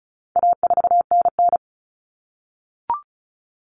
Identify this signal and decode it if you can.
Morse code, heard as a keyed tone, about 34 wpm, 700 Hz. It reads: A4NN
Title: Common Cut Numbers - 4-Digit 35wpm